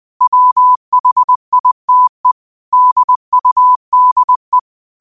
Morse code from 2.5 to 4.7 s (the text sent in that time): DUDE